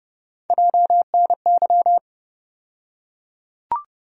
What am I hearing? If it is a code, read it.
Morse code, heard as a keyed tone, about 30 wpm, 700 Hz. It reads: JNY
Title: USA Call Sign Postfixes - 3 Letters 30wpm